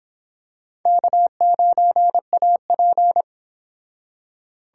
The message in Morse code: K9AP

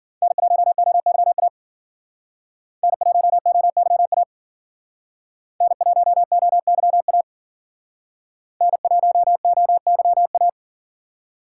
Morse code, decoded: N1OYA N1OYA N1OYA N1OYA

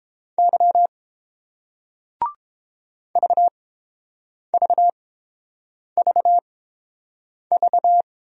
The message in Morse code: Y VVVV

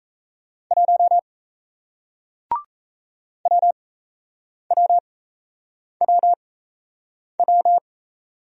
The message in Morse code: 1 WWWW

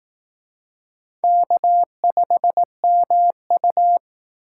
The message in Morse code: K5MU

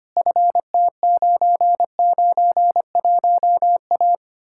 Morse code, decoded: FT991A

